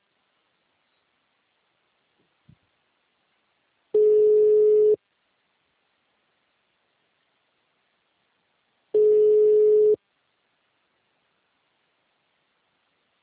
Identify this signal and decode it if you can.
Ringback tone (Europe, 425 Hz).
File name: out- -2020- - - .18231